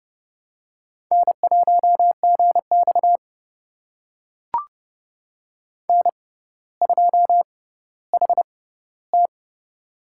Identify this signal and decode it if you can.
Morse code, heard as a keyed tone, about 30 wpm, spaced as for 15 wpm, 700 Hz. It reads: N1GX N2HT
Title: USA Call Signs - 4 Characters - 6x Character Spacing - 30wpm